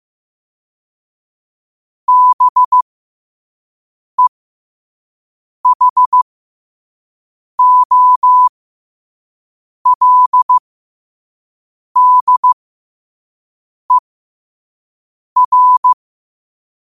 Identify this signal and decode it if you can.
Morse code, heard as a keyed tone, about 15 wpm, spaced as for 5 wpm, 1000 Hz. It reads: BEHOLDER
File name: qotdfiles / 5wpm